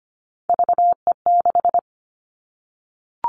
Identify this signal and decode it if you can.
Morse code, heard as a keyed tone, about 25 wpm, 700 Hz. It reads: VE6
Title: Callsign Prefixes for Top 25 DXCC - Repeated 3x - 25wpm